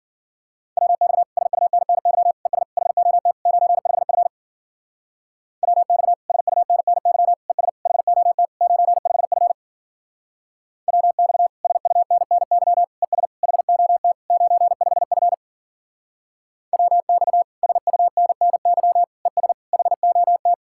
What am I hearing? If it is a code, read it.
Morse code, heard as a keyed tone, about 50 wpm, 700 Hz. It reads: WX SUNNY ES HOT 95F WX SUNNY ES HOT 95F WX SUNNY ES HOT 95F WX SUNNY ES HOT